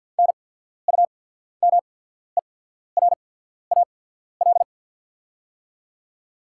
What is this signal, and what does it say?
Morse code, heard as a keyed tone, about 48 wpm, spaced as for 14 wpm, 700 Hz. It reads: NUMERAL
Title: 8x Character Spacing - Top 500 Words - 50wpm